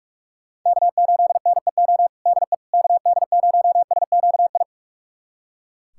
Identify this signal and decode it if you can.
Morse code, heard as a keyed tone, about 41 wpm, 700 Hz. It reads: K8NEO DE KD0SQI
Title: USA Callsign DE Callsign 45wpm